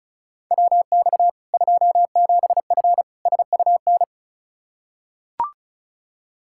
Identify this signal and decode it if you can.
Morse code, heard as a keyed tone, about 36 wpm, 700 Hz. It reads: WX 27F SUN
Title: Practice QSO - Exchange 2 - Info - 35wpm